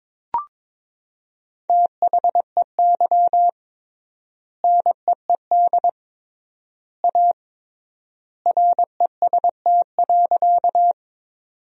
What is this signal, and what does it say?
Morse code, heard as a keyed tone, about 22 wpm, 700 Hz. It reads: THEY NEED A REST.
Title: 3x Word-Spacing - Sentences from Top 500 Words - 22wpm